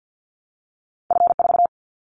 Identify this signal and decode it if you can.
Morse code, heard as a keyed tone, about 50 wpm, 700 Hz. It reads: F4
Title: Callsign Prefixes for Top 50 DXCC - 50wpm